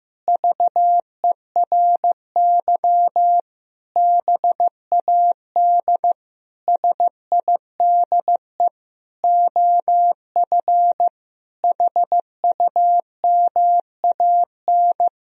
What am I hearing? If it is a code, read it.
Morse code, heard as a keyed tone, about 15 wpm, 700 Hz. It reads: VERY BAD SIDE OF HUMAN
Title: Sentences from Top 1000 Words - Review Part 3 - 15wpm